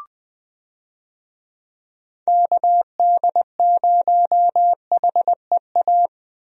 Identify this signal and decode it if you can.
Morse code, heard as a keyed tone, about 20 wpm, 700 Hz. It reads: KD0HEA
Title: USA Call Signs - 6 Characters - 20wpm